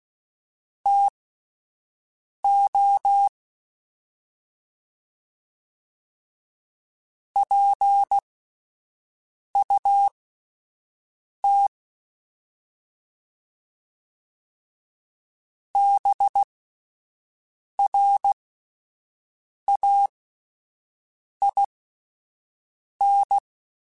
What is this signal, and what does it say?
Morse code, heard as a keyed tone, about 16 wpm, spaced as for 5 wpm, 800 Hz. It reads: TO PUT BRAIN